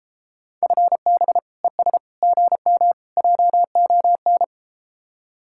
Morse code, heard as a keyed tone, about 33 wpm, 700 Hz. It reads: FB ES GM JON